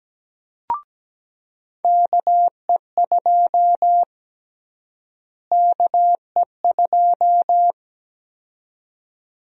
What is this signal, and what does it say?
Morse code, heard as a keyed tone, about 17 wpm, 700 Hz. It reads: KE2 KE2
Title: USA Call Sign Prefixes - Repeated 2x - 17wpm